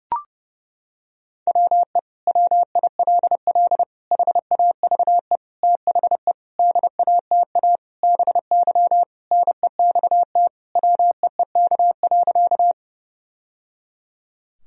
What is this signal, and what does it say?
Morse code, heard as a keyed tone, about 30 wpm, 700 Hz. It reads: WE WILL HAVE THE DATA BY NEXT WEEK.
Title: Sentences from Top 1000 Words - Review Part 3 - 30wpm